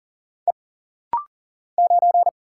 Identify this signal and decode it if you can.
Morse code, heard as a keyed tone, about 40 wpm, 700 Hz. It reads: E 9
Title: Single Letter-Number Alt - Rapid-Fire 40wpm